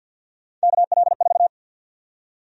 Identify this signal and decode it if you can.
Morse code, heard as a keyed tone, about 51 wpm, 700 Hz. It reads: KL4
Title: USA Call Sign Prefixes - Repeated 2x - 50wpm